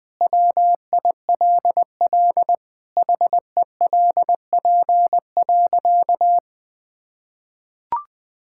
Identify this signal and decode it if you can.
Morse code, heard as a keyed tone, about 20 wpm, 700 Hz. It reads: WILL HELP.
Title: Sentences from Top 1000 Words - Review Part 5 - 20wpm